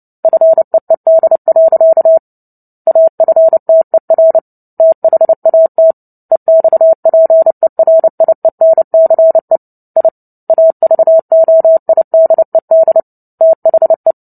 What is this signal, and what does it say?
Morse code, heard as a keyed tone, about 30 wpm, 650 Hz. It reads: FEED. AFTER THAT EXPERIENCE I AVOIDED THE
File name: Book Two / 03-Chapter 3 - 29 WPM